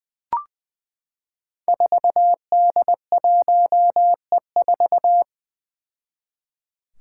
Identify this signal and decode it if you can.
Morse code, heard as a keyed tone, about 20 wpm, 700 Hz. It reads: 4D1E4
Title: Five Challenging Letter-Number Combinations 20wpm